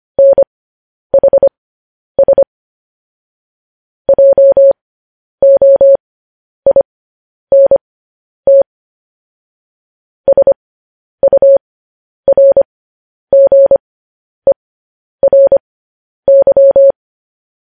Morse code, heard as a keyed tone, about 25 wpm, spaced as for 10 wpm, 550 Hz. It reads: NHS JOINT SURGERY
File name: podcast / News headlines-1x-25x10